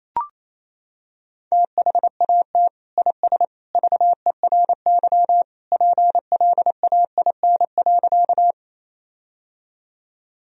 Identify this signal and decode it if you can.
Morse code, heard as a keyed tone, about 28 wpm, 700 Hz. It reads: THAT IS VERY PLAIN.
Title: Sentences from Top 1000 Words - Review Part 2 - 28wpm